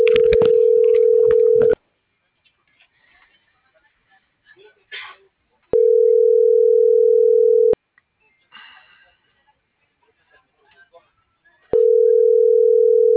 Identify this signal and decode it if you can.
Ringback tone (North America), 440 + 480 Hz.